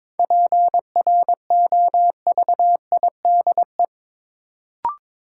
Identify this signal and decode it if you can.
Morse code, heard as a keyed tone, about 22 wpm, 700 Hz. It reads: PROVIDE
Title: Sentences from Top 1000 Words - Review Part 2 - New Word Drill - 22wpm